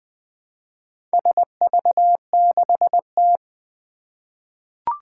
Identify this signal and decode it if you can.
Morse code, heard as a keyed tone, about 20 wpm, 700 Hz. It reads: SV6T
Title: Four Challenging Letter-Number Combinations 20wpm